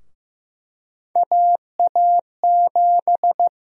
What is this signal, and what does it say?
Morse code, heard as a keyed tone, about 15 wpm, 700 Hz. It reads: AA7
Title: USA Call Sign Prefixes - 15wpm